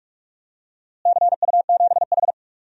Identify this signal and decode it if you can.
Morse code, heard as a keyed tone, about 43 wpm, 700 Hz. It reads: CU7H
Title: DXCC-100 Call Signs - 4 Characters - Repeated 3x - 45wpm